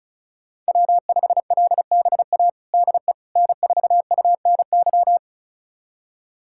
Morse code, heard as a keyed tone, about 36 wpm, 700 Hz. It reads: W5LBA DE N4UNY
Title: USA Callsign DE Callsign - 5 Characters - 35wpm